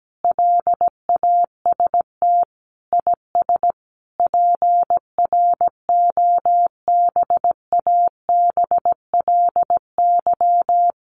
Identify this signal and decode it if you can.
Morse code, heard as a keyed tone, about 17 wpm, 700 Hz. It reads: LAST IS PROBABLY
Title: Sentences from Top 700 Words 17wpm